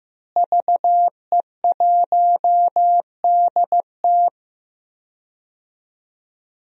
Morse code, heard as a keyed tone, about 15 wpm, 700 Hz. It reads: VE1DT
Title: DXCC-100 Call Signs - 5 Characters - Repeated 3x - 15wpm